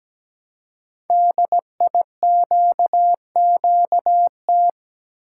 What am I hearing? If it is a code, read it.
Morse code, heard as a keyed tone, about 17 wpm, 700 Hz. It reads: DIQQT
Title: Code Groups - 5 Letters - 17wpm